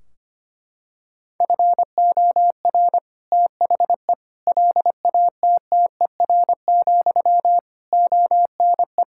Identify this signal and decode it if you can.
Morse code, heard as a keyed tone, about 25 wpm, 700 Hz. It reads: FOR THE LATTER, ONE
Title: Sentences from Top 1600 Words 25wpm